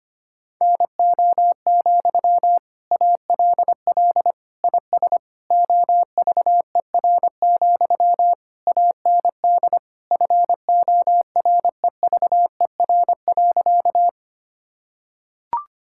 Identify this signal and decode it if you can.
Morse code, heard as a keyed tone, about 25 wpm, 700 Hz. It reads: NO, ALL IS OVER, AND FOREVER.
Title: Sentences from Top 1600 Words 25wpm